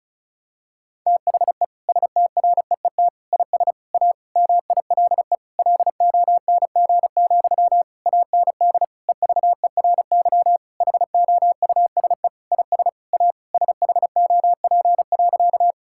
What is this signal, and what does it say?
Morse code, heard as a keyed tone, about 34 wpm, 700 Hz. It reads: THE STREET IS A MILE LONG, AND EVERY HOUSE IS A SHOP.
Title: Sentences from Top 1600 Words 35wpm